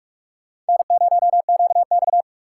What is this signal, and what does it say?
Morse code, heard as a keyed tone, about 59 wpm, spaced as for 52 wpm, 700 Hz. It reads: N0QX
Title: Speed Racing - USA 1x2 Call Signs 45wpm